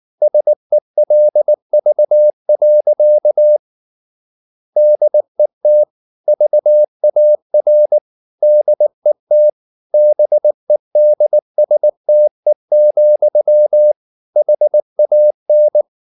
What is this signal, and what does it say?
Morse code, heard as a keyed tone, about 19 wpm, 600 Hz. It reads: SELV. DET VAR DET BEDSTE, HAN
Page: Ferie 19 wpm